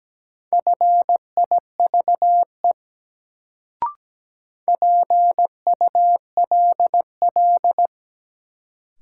Morse code, heard as a keyed tone, about 17 wpm, 700 Hz. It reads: FIVE PULL
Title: Sentences from Top 400 Words - New Word Drill - 17wpm